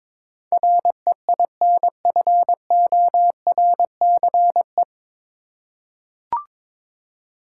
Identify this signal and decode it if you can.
Morse code, heard as a keyed tone, about 22 wpm, 700 Hz. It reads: REINFORCE
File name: 9 Letter Words - First 3 Letters Given - 22wpm